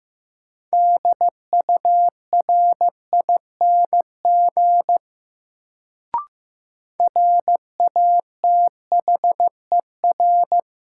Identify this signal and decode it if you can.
Morse code, heard as a keyed tone, about 15 wpm, 700 Hz. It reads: DURING RATHER
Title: Sets of 2 Words - New Word Drill - 15wpm